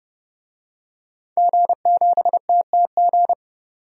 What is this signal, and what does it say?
Morse code, heard as a keyed tone, about 31 wpm, 700 Hz. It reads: G7TTG
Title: DXCC-100 Call Signs - 5 Characters - 30wpm